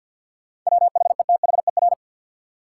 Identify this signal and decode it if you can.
Morse code, heard as a keyed tone, about 52 wpm, 700 Hz. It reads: WHETHER
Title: Sets of 3 Words - New Word Drill - 50wpm